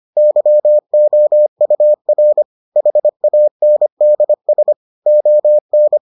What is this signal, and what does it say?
Morse code, heard as a keyed tone, about 25 wpm, 600 Hz. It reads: YOUR HANDS ON